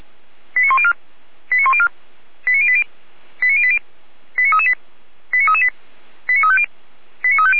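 Five-tone selective call (ZVEI), heard as quick runs of five tones, tones of about 70 ms. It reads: 89104 89104 89080 89080 89338 89338 89350 89350
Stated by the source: ZVEI five-tone call 89304 received via SDR.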